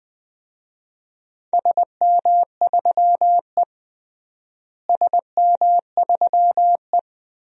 Morse code, heard as a keyed tone, about 21 wpm, 700 Hz. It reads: SM3E SM3E
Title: DXCC-100 Call Signs - 4 Characters - Repeated 3x - 20wpm